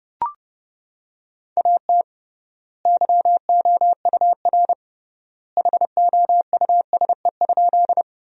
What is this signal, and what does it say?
Morse code, heard as a keyed tone, about 30 wpm, 700 Hz. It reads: AT YOUR HOUSE?
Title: 3x Word-Spacing - Sentences from Top 200 Words - 30wpm